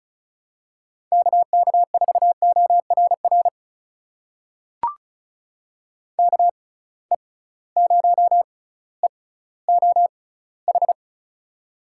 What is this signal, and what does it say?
Morse code, heard as a keyed tone, about 34 wpm, spaced as for 18 wpm, 700 Hz. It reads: KK4ORR KE0EOH
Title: USA Call Signs - 6 Characters - 6x Character Spacing - 35wpm